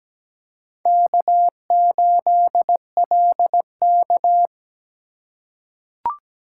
Morse code, heard as a keyed tone, about 17 wpm, 700 Hz. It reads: K8LK
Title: USA Call Signs - 4 Characters - 2x Character Spacing - 17wpm